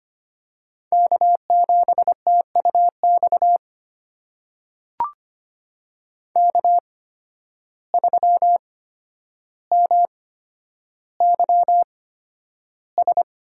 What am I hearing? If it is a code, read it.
Morse code, heard as a keyed tone, about 25 wpm, spaced as for 11 wpm, 700 Hz. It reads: K7TUX K3MYS
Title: USA Call Signs - 5 Characters - 8x Character Spacing - 25wpm